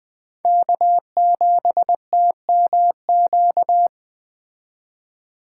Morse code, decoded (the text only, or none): K7TMQ